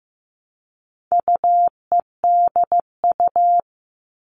Morse code, heard as a keyed tone, about 15 wpm, 700 Hz. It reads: UEDU
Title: All Cut Numbers - 4-Digit 15wpm